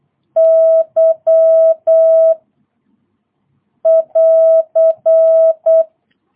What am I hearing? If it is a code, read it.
Morse code, heard as a keyed tone, about 8 wpm, 650 Hz. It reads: Y <AR>
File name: jh8cbh / 2022cw